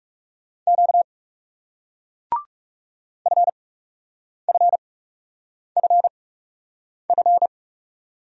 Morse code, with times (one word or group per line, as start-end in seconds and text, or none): 0.7-1.0 Q
3.3-7.5 FFFF